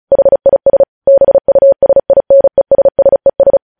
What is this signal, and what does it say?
Morse code, heard as a keyed tone, about 36 wpm, 550 Hz. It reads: HIS BUSINESSES
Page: news headlines in morse code at 35 wpm